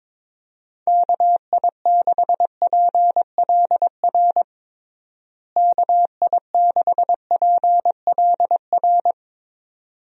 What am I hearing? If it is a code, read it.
Morse code, heard as a keyed tone, about 22 wpm, 700 Hz. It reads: KI6PLR KI6PLR
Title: USA Call Signs - 6 Characters - Repeated 2x - 22wpm